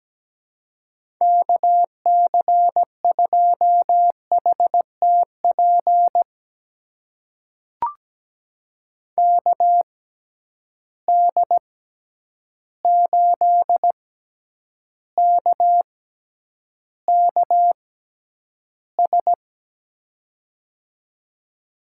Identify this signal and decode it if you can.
Morse code, heard as a keyed tone, about 17 wpm, spaced as for 9 wpm, 700 Hz. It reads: KC2HTP KD8KKS
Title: USA Call Signs - 6 Characters - 6x Character Spacing - 17wpm